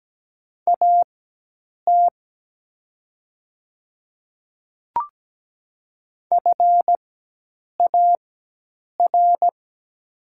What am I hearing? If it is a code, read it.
Morse code, heard as a keyed tone, about 17 wpm, spaced as for 8 wpm, 700 Hz. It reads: AT FAR
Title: 4x Character Spacing - Top 500 Words - 17wpm